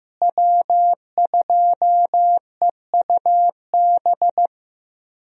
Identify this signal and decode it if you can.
Morse code, heard as a keyed tone, about 15 wpm, 700 Hz. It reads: W2EUB
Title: USA Call Signs - 5 Characters - Repeated 3x - 15wpm